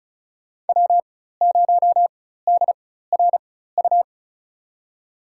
Morse code, heard as a keyed tone, about 36 wpm, spaced as for 16 wpm, 700 Hz. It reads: W0DRU
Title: USA Call Signs - 5 Characters - 4x Character Spacing - 35wpm